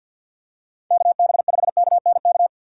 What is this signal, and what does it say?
Morse code, heard as a keyed tone, about 50 wpm, 700 Hz. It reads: KB5KNK
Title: USA Call Signs - 6 Characters - 3x Character Spacing - 50wpm